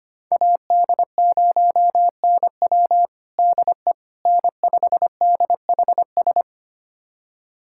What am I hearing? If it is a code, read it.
Morse code, heard as a keyed tone, about 25 wpm, 700 Hz. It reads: AD0NW DE N5DHS
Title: USA Callsign DE Callsign - 5 Characters - 25wpm